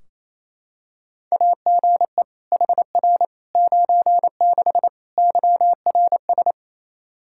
Morse code, heard as a keyed tone, about 28 wpm, 700 Hz. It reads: AGE HR 96 YRS
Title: K1USN SST QSO Practice 28wpm